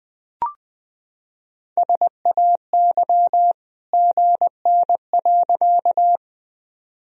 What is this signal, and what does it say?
Morse code, heard as a keyed tone, about 21 wpm, 700 Hz. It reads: SAY GN.
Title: Sentences from Top 300 Words and Abbreviations 20wpm